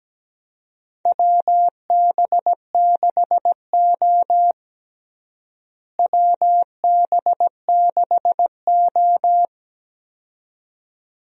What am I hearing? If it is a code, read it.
Morse code, heard as a keyed tone, about 17 wpm, 700 Hz. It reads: WB6O WB6O
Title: USA Call Signs - 4 Characters - Repeated 2x - 17wpm